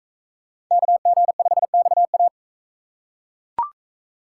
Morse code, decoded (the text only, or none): KG5XA